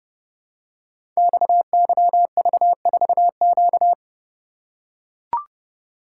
Morse code, heard as a keyed tone, about 30 wpm, 700 Hz. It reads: XYV4Q
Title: Code Groups - 5 Letter:Numbers - 30wpm